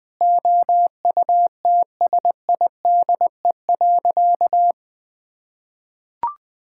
Morse code, heard as a keyed tone, about 21 wpm, 700 Hz. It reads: OUTSIDE.
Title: Sentences with 5-Word Phrases - Spoken Beginning - 20wpm